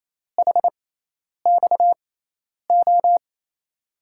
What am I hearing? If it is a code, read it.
Morse code, heard as a keyed tone, about 27 wpm, spaced as for 10 wpm, 700 Hz. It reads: HXO
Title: USA Call Signs - 5 Characters - 6x Character Spacing - 28wpm